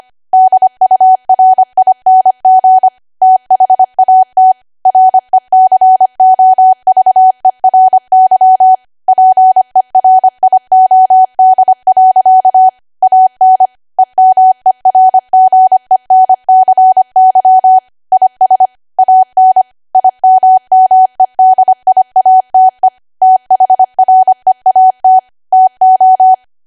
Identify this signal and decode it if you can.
Morse code, heard as a keyed tone, about 25 wpm, 750 Hz. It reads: DURING THAT RECOVERY PERIOD. AN EMERGENCY IS AN IMMEDIATE THREAT TO
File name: Archive / 25 WPM / 25WPM